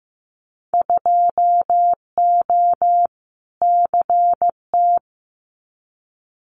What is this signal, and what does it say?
Morse code, heard as a keyed tone, about 15 wpm, 700 Hz. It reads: 2O CT
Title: Winter Field Day Exchanges 15wpm